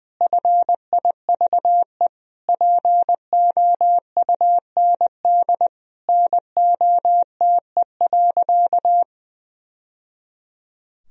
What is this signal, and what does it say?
Morse code, heard as a keyed tone, about 20 wpm, 700 Hz. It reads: FIVE POUND NOTE.
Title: Sentences from Top 1000 Words - Review Part 2 - 20wpm